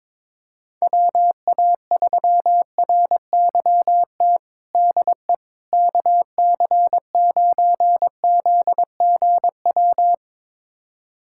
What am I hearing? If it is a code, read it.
Morse code, heard as a keyed tone, about 22 wpm, 700 Hz. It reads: WA3RYT DE KC9ZGW
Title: USA Callsign DE Callsign 22wpm